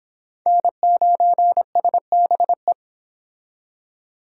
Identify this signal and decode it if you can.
Morse code, heard as a keyed tone, about 26 wpm, 700 Hz. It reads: N9SBE